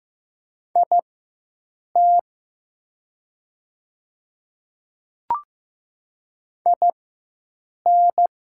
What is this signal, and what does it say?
Morse code, heard as a keyed tone, about 15 wpm, spaced as for 7 wpm, 700 Hz. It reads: IT IN
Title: 4x Character Spacing - Top 100 Words - 15wpm